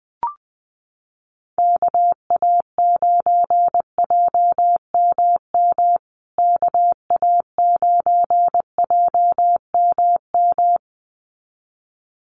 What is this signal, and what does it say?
Morse code, heard as a keyed tone, about 20 wpm, 700 Hz. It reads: KA9JMM KA9JMM
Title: USA Call Signs Repeated 2x 20wpm